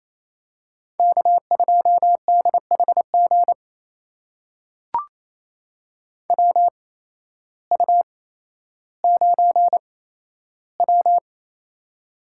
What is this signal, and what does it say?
Morse code, heard as a keyed tone, about 28 wpm, spaced as for 13 wpm, 700 Hz. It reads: K2DHG WU9W
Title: USA Call Signs - 8x Character Spacing - 28wpm